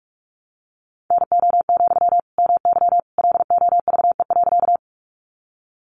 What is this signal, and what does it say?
Morse code, heard as a keyed tone, about 44 wpm, 700 Hz. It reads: NO, MY LOVE.
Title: Sentences from Top 400 Words 45wpm